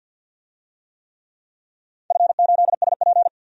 Morse code, heard as a keyed tone, about 51 wpm, 700 Hz. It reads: F8SP
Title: DXCC-100 Call Signs - 4 Characters - 50wpm